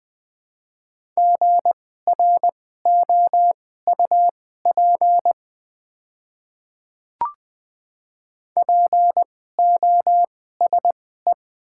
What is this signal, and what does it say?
Morse code, heard as a keyed tone, about 20 wpm, spaced as for 14 wpm, 700 Hz. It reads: GROUP POSE
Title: 2x Character Spacing - Top 500 Words - 20wpm